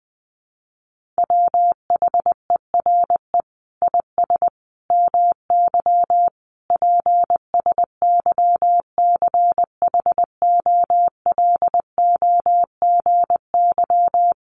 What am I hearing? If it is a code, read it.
Morse code, heard as a keyed tone, about 20 wpm, 700 Hz. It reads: WHERE IS MY PSYCHOLOGY
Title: Sentences from Top 1400 Words 20wpm